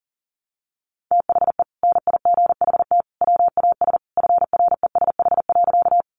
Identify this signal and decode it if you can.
Morse code, heard as a keyed tone, about 40 wpm, 700 Hz. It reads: THE NIGHT WAS FRESH.
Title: Sentences from Top 1200 Words 40wpm